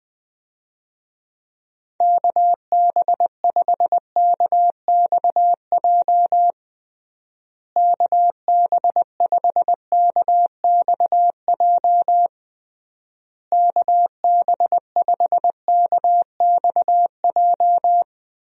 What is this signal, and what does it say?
Morse code, heard as a keyed tone, about 20 wpm, 700 Hz. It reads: KB5KXJ KB5KXJ KB5KXJ
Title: USA Call Signs - 6 Characters - Repeated 3x - 20wpm